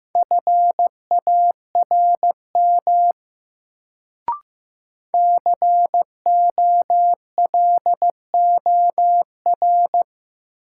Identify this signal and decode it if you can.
Morse code, heard as a keyed tone, about 15 wpm, 700 Hz. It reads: FARM COLOR
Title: Sentences from Top 1000 Words - Review Part 2 - New Word Drill - 15wpm